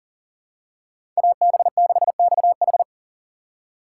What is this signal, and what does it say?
Morse code, heard as a keyed tone, about 40 wpm, 700 Hz. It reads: AB6XH
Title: USA Call Signs - 5 Characters - 6x Character Spacing - 40wpm